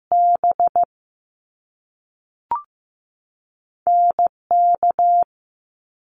Morse code, heard as a keyed tone, about 15 wpm, spaced as for 5 wpm, 700 Hz. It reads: BNK
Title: Top Two Letter Combinations in Words 15wpm